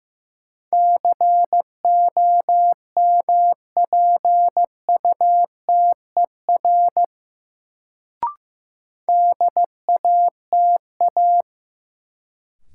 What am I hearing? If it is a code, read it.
Morse code, heard as a keyed tone, about 15 wpm, 700 Hz. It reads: COMPUTER DATA
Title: Sentences from Top 1000 Words - Review Part 3 - New Word Drill - 15wpm